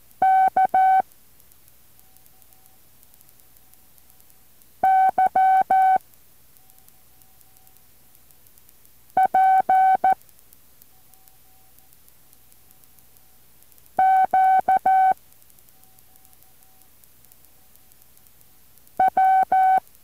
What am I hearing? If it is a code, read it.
Morse code, heard as a keyed tone, about 14 wpm, 760 Hz. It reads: K Y P Q W